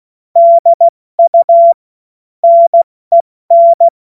Morse code, heard as a keyed tone, about 16 wpm, spaced as for 14 wpm, 680 Hz. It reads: DU NEN
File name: Morsekurs / Witze / Witz 06. Juli 2021, 70 F80